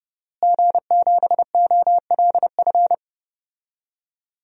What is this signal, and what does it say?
Morse code, heard as a keyed tone, about 32 wpm, 700 Hz. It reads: G7OLF
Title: DXCC-100 Call Signs - 5 Characters - Repeated 2x - 30wpm